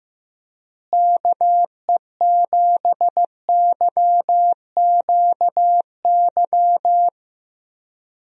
Morse code, heard as a keyed tone, about 16 wpm, 700 Hz. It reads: KE7YQY
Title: USA Call Signs - 6 Characters - Repeated 3x - 15wpm